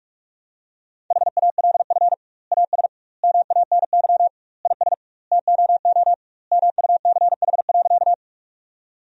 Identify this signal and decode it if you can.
Morse code, heard as a keyed tone, about 48 wpm, 700 Hz. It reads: HALF AS MANY IS TOO MUCH.
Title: 2x Word-Spacing - Sentences from Top 400 Words - 45wpm